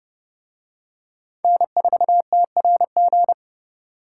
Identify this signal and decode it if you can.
Morse code, heard as a keyed tone, about 29 wpm, 700 Hz. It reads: N4TRG